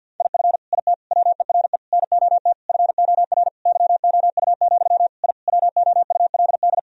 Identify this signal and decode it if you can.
Morse code, heard as a keyed tone, about 50 wpm, 700 Hz. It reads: IF IT WERE NOT FOR YOU, I WOULD
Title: Sentences from Top 1000 Words - Review Part 2 - 50wpm